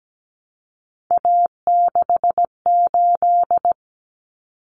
Morse code, heard as a keyed tone, about 17 wpm, 700 Hz. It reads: A68
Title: Common Cut Numbers - 3-Digit 17wpm